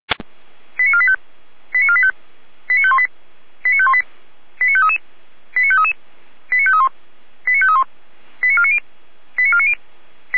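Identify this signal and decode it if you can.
Five-tone selective call (ZVEI), heard as quick runs of five tones, tones of about 70 ms. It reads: 89485 89485 89518 89518 89533 89533 89531 89531 89490 89490
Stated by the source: ZVEI five-tone selective call received by SDR.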